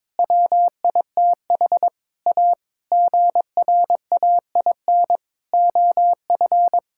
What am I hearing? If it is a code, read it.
Morse code, heard as a keyed tone, about 22 wpm, 700 Hz. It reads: WITH A GRAIN OF
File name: practice / 22wpm / Sentences with 8-Word Phrases - 22wpm